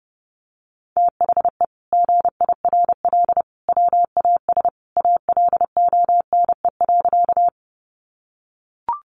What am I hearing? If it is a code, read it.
Morse code, heard as a keyed tone, about 30 wpm, 700 Hz. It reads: THE GIRL WAS ALONE.
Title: Sentences from Top 1000 Words 30wpm